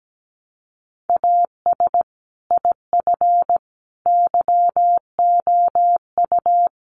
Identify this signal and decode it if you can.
Morse code, heard as a keyed tone, about 17 wpm, 700 Hz. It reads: AS IF YOU
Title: Sentences from Top 1300 Words 17wpm